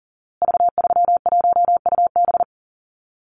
Morse code, heard as a keyed tone, about 40 wpm, 700 Hz. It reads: V31UB
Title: Call Signs Repeated 2x - 40wpm